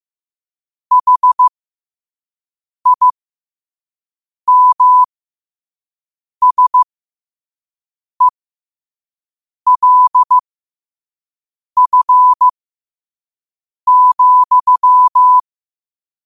Morse code, heard as a keyed tone, about 15 wpm, spaced as for 5 wpm, 1000 Hz. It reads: HIMSELF,
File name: qotdfiles / 5wpm